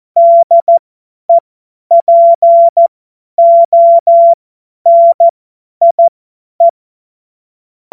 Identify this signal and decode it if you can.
Morse code, heard as a keyed tone, about 14 wpm, spaced as for 10 wpm, 680 Hz. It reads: DEPONIE
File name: Morsekurs / Witze / Witz 03 Juni 2025 50 F70